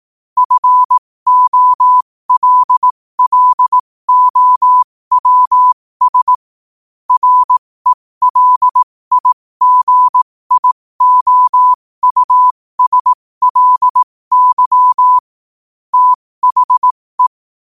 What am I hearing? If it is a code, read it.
Morse code, heard as a keyed tone, about 18 wpm, spaced as for 15 wpm, 1000 Hz. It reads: FOLLOWS RELIGIOUSLY THE
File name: qotdfiles / 15wpm